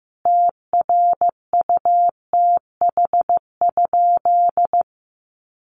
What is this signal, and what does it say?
Morse code, heard as a keyed tone, about 15 wpm, 700 Hz. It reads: TRUTH?
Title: Sentences from Top 1300 Words 15wpm